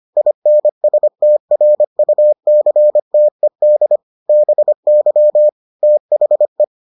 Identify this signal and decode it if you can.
Morse code, heard as a keyed tone, about 25 wpm, 600 Hz. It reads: INSTRUCTED BY THE